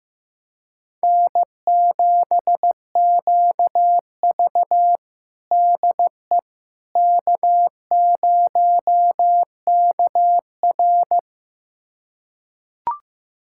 Morse code, heard as a keyed tone, about 15 wpm, 700 Hz. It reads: N7QV DE K0KR
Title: USA Callsign DE Callsign - 4 Characters - 15wpm